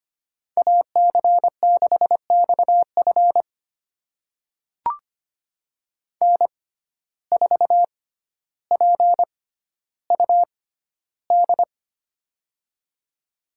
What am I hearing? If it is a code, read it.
Morse code, heard as a keyed tone, about 25 wpm, spaced as for 13 wpm, 700 Hz. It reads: AC6XF N4PUD
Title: USA Call Signs - 5 Characters - 6x Character Spacing - 25wpm